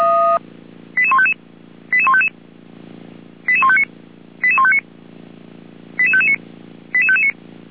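Five-tone selective call (ZVEI), heard as quick runs of five tones, tones of about 70 ms. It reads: 88155 88155 88159 88159 88559 88559
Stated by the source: ZVEI five-tone selective call received on SDR, code 88080.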